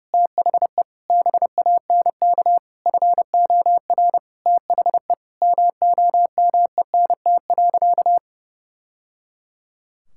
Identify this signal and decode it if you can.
Morse code, heard as a keyed tone, about 30 wpm, 700 Hz. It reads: THE BANK FOR THE MOMENT.
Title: Sentences from Top 1000 Words - Review Part 6 - 30wpm